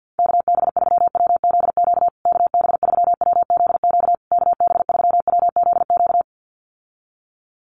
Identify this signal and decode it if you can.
Morse code, heard as a keyed tone, about 50 wpm, 700 Hz. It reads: KB3WZQ KB3WZQ KB3WZQ
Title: USA Call Signs Repeated 3x 50wpm